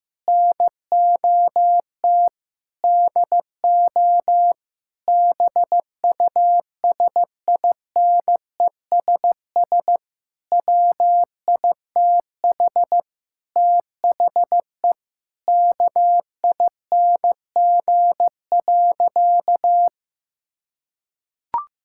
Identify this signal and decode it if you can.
Morse code, heard as a keyed tone, about 15 wpm, 700 Hz. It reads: NOT DO BUSINESS WITH THE KING.
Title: Sentences from Top 1000 Words - Review Part 6 - 15wpm